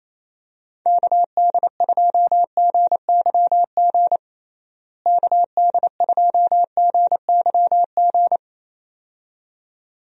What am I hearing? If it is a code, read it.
Morse code, heard as a keyed tone, about 28 wpm, 700 Hz. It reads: KD2GYG KD2GYG
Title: USA Call Signs - 6 Characters - Repeated 2x - 28wpm